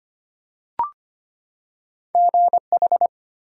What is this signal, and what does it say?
Morse code, heard as a keyed tone, about 25 wpm, 700 Hz. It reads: GH
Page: morse-code-ninja-practice-25wpm